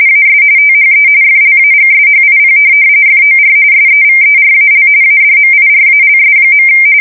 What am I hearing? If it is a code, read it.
RTTY, about 45.45 baud. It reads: AS THE FLAGSHIP FOR SEVERAL ADMIRALS.
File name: fsk170 / USS-Missouri